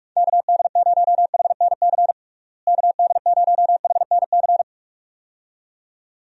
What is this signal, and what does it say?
Morse code, heard as a keyed tone, about 44 wpm, 700 Hz. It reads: KD0HNC KD0HNC
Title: USA Call Signs - 6 Characters - Repeated 2x - 45wpm